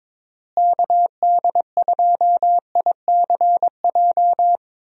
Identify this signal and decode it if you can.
Morse code, heard as a keyed tone, about 22 wpm, 700 Hz. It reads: KD2ICJ
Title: USA Call Signs - 6 Characters - Repeated 3x - 22wpm